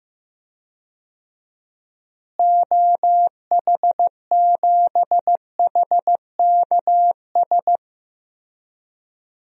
Morse code, heard as a keyed tone, about 15 wpm, 700 Hz. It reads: OH7HKS
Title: DXCC-100 Call Signs - 6 Characters - Repeated 3x - 15wpm